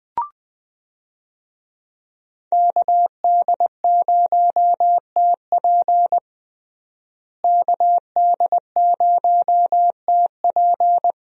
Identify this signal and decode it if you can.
Morse code, heard as a keyed tone, about 20 wpm, 700 Hz. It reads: KD0TP KD0TP
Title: USA Call Signs - 5 Characters - Repeated 3x - 20wpm